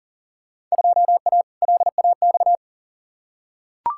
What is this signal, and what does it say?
Morse code, heard as a keyed tone, about 40 wpm, 700 Hz. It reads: 2A LAX
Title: Field Day Exchanges - 40wpm